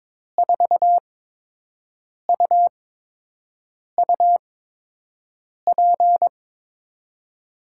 Morse code, heard as a keyed tone, about 22 wpm, spaced as for 6 wpm, 700 Hz. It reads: 4UUP
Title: USA Call Signs - 6 Characters - 8x Character Spacing - 22wpm